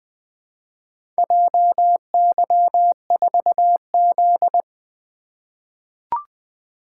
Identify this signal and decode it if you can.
Morse code, heard as a keyed tone, about 20 wpm, 700 Hz. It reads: JY4Z
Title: Four Letter-Number Combinations - 20wpm